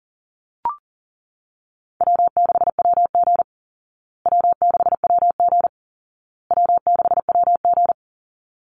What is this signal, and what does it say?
Morse code, heard as a keyed tone, about 40 wpm, 700 Hz. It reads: W6WG W6WG W6WG
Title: Call Signs Repeated 2x - 40wpm